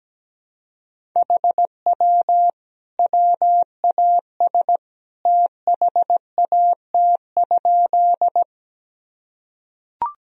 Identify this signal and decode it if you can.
Morse code, heard as a keyed tone, about 17 wpm, 700 Hz. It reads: HW WAS THAT?
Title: Sentences from Top 100 Words and Abbreviations 17wpm